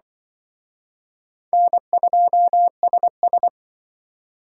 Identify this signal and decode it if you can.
Morse code, heard as a keyed tone, about 24 wpm, 700 Hz. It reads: N2SS
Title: Speed Racing - USA 1x2 Call Signs 20wpm